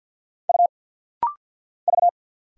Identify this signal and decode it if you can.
Morse code, heard as a keyed tone, about 50 wpm, 700 Hz. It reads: U V
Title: Single Letters - Rapid-Fire 50wpm